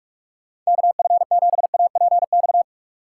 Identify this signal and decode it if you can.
Morse code, heard as a keyed tone, about 47 wpm, 700 Hz. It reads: KF7APX